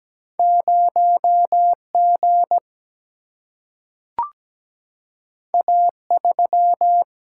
Morse code, heard as a keyed tone, about 17 wpm, 700 Hz. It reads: 0 G A 3
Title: Single-Letter Single-Number 17wpm